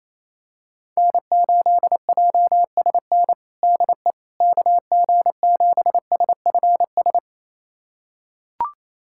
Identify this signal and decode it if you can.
Morse code, heard as a keyed tone, about 28 wpm, 700 Hz. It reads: N8JSN DE KG7SFS
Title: USA Callsign DE Callsign 28wpm